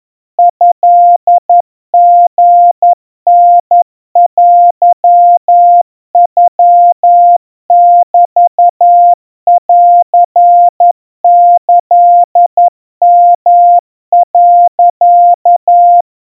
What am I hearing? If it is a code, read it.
Morse code (Japanese Wabun code), heard as a keyed tone, about 11 wpm, 700 Hz. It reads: トリタテノメンキヨ、
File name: kawa / Morse sample / kamyu 40 cpm 700 Hz Crispy